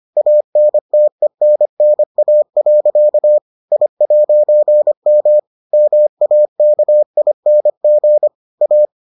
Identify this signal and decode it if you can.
Morse code, heard as a keyed tone, about 25 wpm, 600 Hz. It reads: ANTENNA. I'M MAKING A